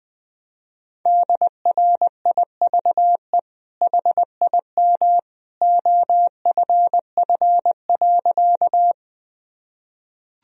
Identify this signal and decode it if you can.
Morse code, heard as a keyed tone, about 20 wpm, 700 Hz. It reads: DRIVE HIM OFF.
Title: Sentences from Top 1000 Words - Review Part 2 - 20wpm